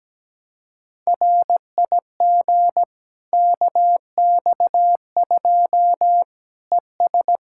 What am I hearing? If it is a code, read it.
Morse code, heard as a keyed tone, about 17 wpm, 700 Hz. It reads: RIG KX2 ES
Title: K1USN SST QSO Practice 17wpm